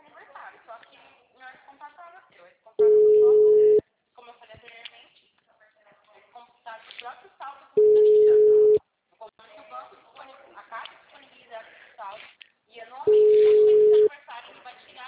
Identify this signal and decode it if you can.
Ringback tone (Europe, 425 Hz), with speech.